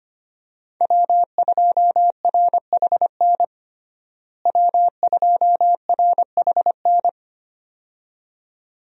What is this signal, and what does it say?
Morse code, heard as a keyed tone, about 25 wpm, 700 Hz. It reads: W2RHN W2RHN
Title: USA Call Signs - 5 Characters - Repeated 2x - 25wpm